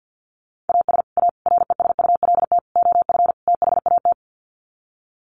Morse code, heard as a keyed tone, about 50 wpm, 700 Hz. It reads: AS A RESULT OF THAT
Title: Sets of 5 Words Encore 50wpm